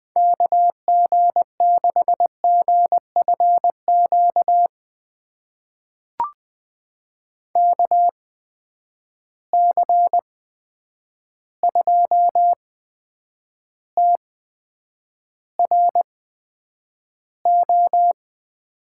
Morse code, heard as a keyed tone, about 20 wpm, spaced as for 9 wpm, 700 Hz. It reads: KG6GFQ KC2TRO
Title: USA Call Signs - 6 Characters - 8x Character Spacing - 20wpm